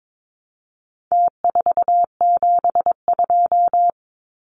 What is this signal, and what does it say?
Morse code, heard as a keyed tone, about 22 wpm, 700 Hz. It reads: T472
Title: Numbers - Contest Style - 4-Digit 22wpm